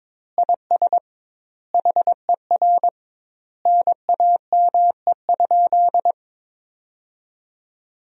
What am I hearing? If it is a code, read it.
Morse code, heard as a keyed tone, about 21 wpm, 700 Hz. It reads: IS HER NAME?
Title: 2x Word-Spacing - Sentences from Top 200 Words - 22wpm